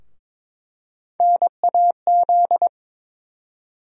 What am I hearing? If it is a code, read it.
Morse code, heard as a keyed tone, about 22 wpm, 700 Hz. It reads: NAZ
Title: USA Call Sign Postfixes - 3 Letters 22wpm